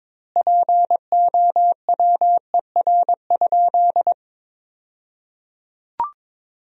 Morse code, heard as a keyed tone, about 22 wpm, 700 Hz. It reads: POWER?
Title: 2x Word-Spacing - Sentences from Top 500 Words - 22wpm